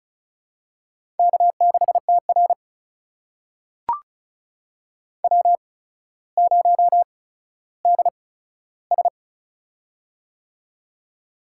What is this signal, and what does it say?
Morse code, heard as a keyed tone, about 36 wpm, spaced as for 15 wpm, 700 Hz. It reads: K6TR W0DS
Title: USA Call Signs - 4 Characters - 8x Character Spacing - 35wpm